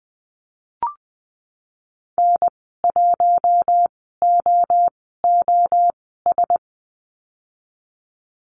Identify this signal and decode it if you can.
Morse code, heard as a keyed tone, about 19 wpm, spaced as for 14 wpm, 700 Hz. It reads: N1OOS
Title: 048 Lesson 9g - Number 4 - Call Signs 20wpm